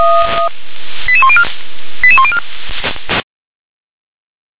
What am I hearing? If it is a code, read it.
Five-tone selective call (ZVEI), heard as quick runs of five tones, tones of about 70 ms. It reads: 88104 88104